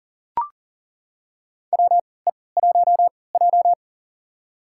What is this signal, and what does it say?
Morse code, heard as a keyed tone, about 40 wpm, spaced as for 23 wpm, 700 Hz. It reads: WE1J
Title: USA Call Signs - 4 Characters - 3x Character Spacing - 40wpm